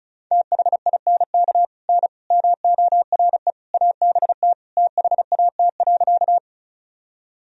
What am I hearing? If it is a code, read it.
Morse code, heard as a keyed tone, about 34 wpm, 700 Hz. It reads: THINK N MORE ABT THAT.
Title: Sentences from Top 200 Words and Abbreviations 35wpm